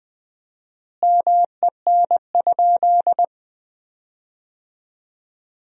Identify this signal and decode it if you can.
Morse code, heard as a keyed tone, about 20 wpm, 700 Hz. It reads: MEN?
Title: 4x Word-Spacing - Sentences from Top 200 Words - 20wpm